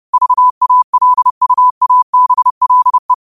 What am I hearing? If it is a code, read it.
Morse code, heard as a keyed tone, about 30 wpm, 1000 Hz. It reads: VALUABLE
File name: qotdfiles / 30wpm